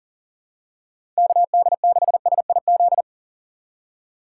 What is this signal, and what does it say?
Morse code, heard as a keyed tone, about 38 wpm, 700 Hz. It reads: KD6SIZ
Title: USA Call Signs - 6 Characters - 2x Character Spacing - 40wpm